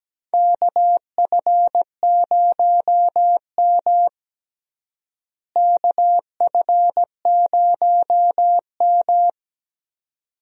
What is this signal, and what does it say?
Morse code, heard as a keyed tone, about 17 wpm, 700 Hz. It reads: KF0M KF0M
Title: USA Call Signs - 4 Characters - Repeated 2x - 17wpm